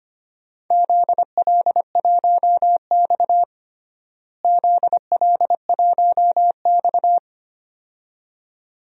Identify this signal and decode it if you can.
Morse code, heard as a keyed tone, about 25 wpm, 700 Hz. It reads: ZL1X ZL1X
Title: DXCC-100 Call Signs - 4 Characters - Repeated 2x - 25wpm